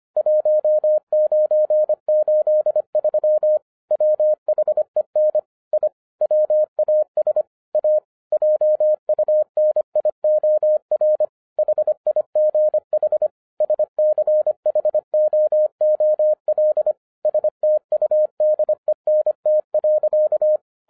Morse code, heard as a keyed tone, about 26 wpm, 610 Hz. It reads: 1983 WHEN I WAS A JUNIOR HIGH SCHOOL STUDENT.
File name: 2011-10-mp3 / n04-JUST MOVED 25wpm